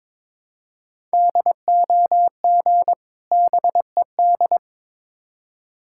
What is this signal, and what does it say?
Morse code, heard as a keyed tone, about 23 wpm, 690 Hz. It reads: DOG BED
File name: Difficult Words with Hints - 22wpm